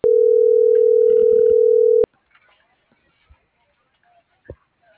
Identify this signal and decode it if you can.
Ringback tone (North America), 440 + 480 Hz.